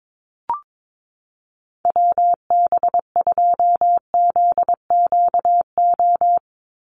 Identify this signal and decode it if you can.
Morse code, heard as a keyed tone, about 22 wpm, 700 Hz. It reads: WB2ZQO
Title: USA Call Signs 22wpm